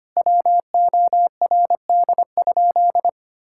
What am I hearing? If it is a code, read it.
Morse code, heard as a keyed tone, about 25 wpm, 700 Hz. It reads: WORD?
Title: Sentences from Top 1000 Words - Review Part 6 - 25wpm